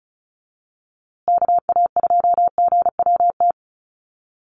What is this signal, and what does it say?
Morse code, heard as a keyed tone, about 34 wpm, 700 Hz. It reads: KA2GWT
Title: USA Call Signs Repeated 2x 35wpm